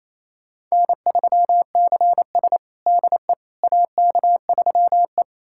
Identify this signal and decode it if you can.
Morse code, heard as a keyed tone, about 28 wpm, 700 Hz. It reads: N3CS DE AK3E
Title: USA Callsign DE Callsign - 4 Characters - 28wpm